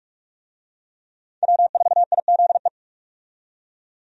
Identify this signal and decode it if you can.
Morse code, heard as a keyed tone, about 47 wpm, 700 Hz. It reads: W4IZE
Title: USA Call Signs - 5 Characters - 45wpm